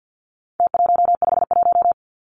Morse code, heard as a keyed tone, about 50 wpm, 700 Hz. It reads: T151
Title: Numbers - Contest Style - 4-Digit 50wpm